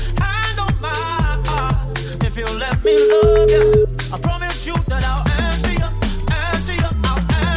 Ringback tone (Europe, 425 Hz), with noise.